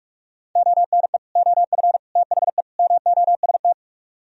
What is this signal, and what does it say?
Morse code, heard as a keyed tone, about 36 wpm, 700 Hz. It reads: ONE OF THE MOST
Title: Speed-Racing - Sets of 4 Words 30wpm